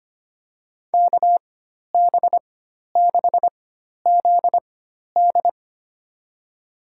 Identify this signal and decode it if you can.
Morse code, heard as a keyed tone, about 25 wpm, spaced as for 12 wpm, 700 Hz. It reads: KB6ZD
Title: USA Call Signs - 4x Character Spacing - 25wpm